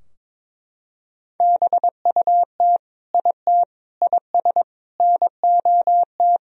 Morse code, heard as a keyed tone, about 21 wpm, 700 Hz. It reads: BUT IT IS NOT
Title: Sentences from Top 1000 Words - Review Part 4 - 22wpm